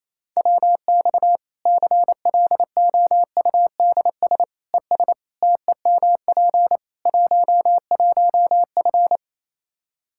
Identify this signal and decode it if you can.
Morse code, heard as a keyed tone, about 27 wpm, 700 Hz. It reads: WX CLOUDS ES TEMP 11F
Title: Practice QSO - Exchange 2 - Info - 28wpm